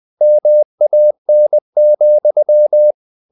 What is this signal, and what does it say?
Morse code, heard as a keyed tone, about 20 wpm, 600 Hz. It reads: MAN,
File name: the old man and the sea part 2 20wpm